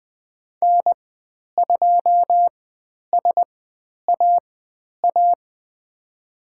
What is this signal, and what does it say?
Morse code, heard as a keyed tone, about 20 wpm, spaced as for 10 wpm, 700 Hz. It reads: N2SAA
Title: 109 Lesson 18g - Letter m - Call Signs 20wpm